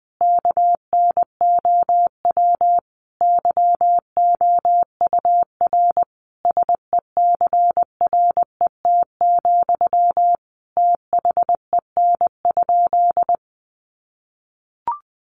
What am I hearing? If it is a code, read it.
Morse code, heard as a keyed tone, about 20 wpm, 700 Hz. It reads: KNOW YOUR SECRET, THEN?
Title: Sentences from Top 1200 Words 20wpm